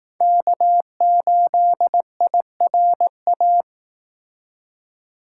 Morse code, heard as a keyed tone, about 19 wpm, 700 Hz. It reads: K8IRA
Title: Speed Racing - USA 1x3 Call Signs 15wpm